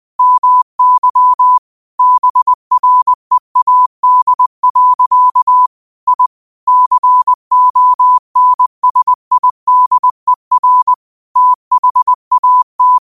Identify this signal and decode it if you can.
Morse code, heard as a keyed tone, about 20 wpm, 1000 Hz. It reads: MY BREAD. I CONSIDER THAT
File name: qotdfiles / 20wpm